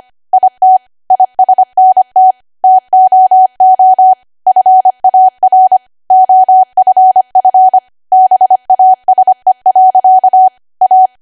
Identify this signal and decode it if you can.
Morse code, heard as a keyed tone, about 25 wpm, 750 Hz. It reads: IT ISNT TOO FAR OFF BASE. A
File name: Archive / 25 WPM / 25WPM